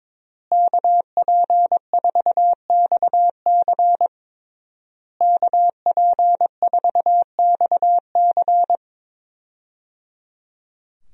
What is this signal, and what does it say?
Morse code, heard as a keyed tone, about 22 wpm, 700 Hz. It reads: KP4XC KP4XC
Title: DXCC-100 Call Signs - 5 Characters - Repeated 3x - 22wpm